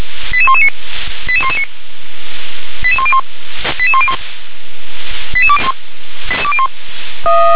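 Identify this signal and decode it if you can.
Five-tone selective call (ZVEI), heard as quick runs of five tones, tones of about 70 ms. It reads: 88119 88119 88101 88101 88201
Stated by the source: ZVEI five-tone selective call 88126, received by SDR.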